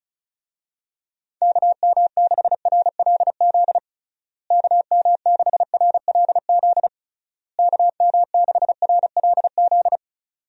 Morse code, heard as a keyed tone, about 36 wpm, 700 Hz. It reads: KM6RLZ KM6RLZ KM6RLZ
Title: USA Call Signs - 6 Characters - Repeated 3x - 35wpm